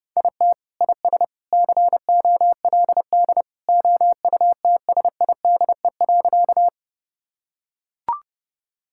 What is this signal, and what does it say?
Morse code, heard as a keyed tone, about 30 wpm, 700 Hz. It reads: IT IS COLD OUTSIDE.
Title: Sentences from Top 1000 Words - Review Part 6 - 30wpm